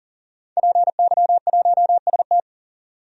Morse code, heard as a keyed tone, about 40 wpm, 700 Hz. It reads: PY1ST
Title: DXCC-100 Call Signs - 5 Characters - 40wpm